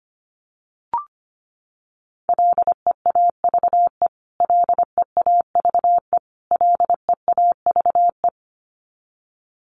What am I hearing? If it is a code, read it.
Morse code, heard as a keyed tone, about 25 wpm, 700 Hz. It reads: LEAVE LEAVE LEAVE
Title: Top 500 Words Repeated 25wpm